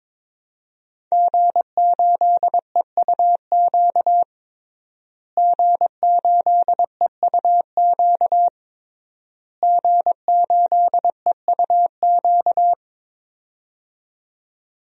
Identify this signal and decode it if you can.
Morse code, heard as a keyed tone, about 22 wpm, 700 Hz. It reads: G8EUQ G8EUQ G8EUQ
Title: DXCC-100 Call Signs - 5 Characters - Repeated 3x - 22wpm